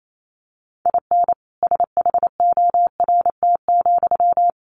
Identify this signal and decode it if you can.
Morse code, heard as a keyed tone, about 28 wpm, 700 Hz. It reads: IN SHORT,
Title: Sentences from Top 1000 Words 28wpm